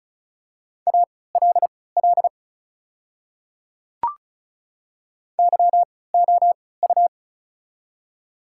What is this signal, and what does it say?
Morse code, heard as a keyed tone, about 36 wpm, spaced as for 20 wpm, 690 Hz. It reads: ALL YOU
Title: 3x Character Spacing - Top 300 Words - 35wpm